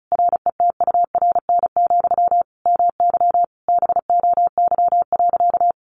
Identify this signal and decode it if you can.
Morse code, heard as a keyed tone, about 34 wpm, 700 Hz. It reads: RETURN, MY BOY.
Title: Sentences from Top 1000 Words 35wpm